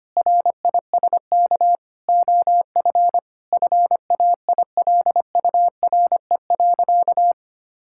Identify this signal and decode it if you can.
Morse code, heard as a keyed tone, about 25 wpm, 700 Hz. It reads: RISK OF FAILURE.
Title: Sentences from Top 1000 Words - Review Part 6 - 25wpm